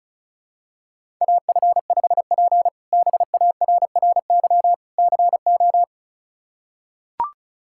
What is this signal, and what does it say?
Morse code, heard as a keyed tone, about 34 wpm, 700 Hz. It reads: AF5P BARRY CO
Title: K1USN SST QSO Practice - 35wpm